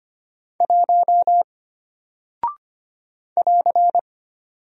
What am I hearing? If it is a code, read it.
Morse code, heard as a keyed tone, about 25 wpm, 700 Hz. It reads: 1 <AR>
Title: Single Character (Letter:Number:Punctuation:Prosign) Alt 25wpm